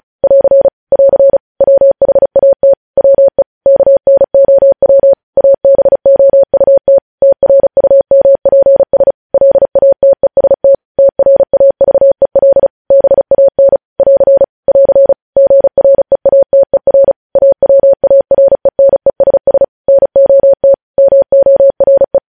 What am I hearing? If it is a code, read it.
Morse code, heard as a keyed tone, about 36 wpm, 550 Hz. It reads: <AR> <AR> WHAT WE KNOW ABOUT TRUMPS LATEST TRAVEL BAN <AR> <AR> GREATER AWARENESS NOT MORE
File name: podcast / News headlines-1x-35x35